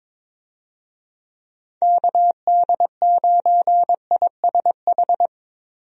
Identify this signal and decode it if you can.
Morse code, heard as a keyed tone, about 22 wpm, 700 Hz. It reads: KD9ISH
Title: USA Call Signs - 6 Characters - Repeated 3x - 22wpm